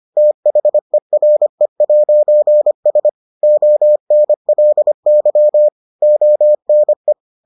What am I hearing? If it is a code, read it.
Morse code, heard as a keyed tone, about 25 wpm, 600 Hz. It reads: THERE'S ONLY ONE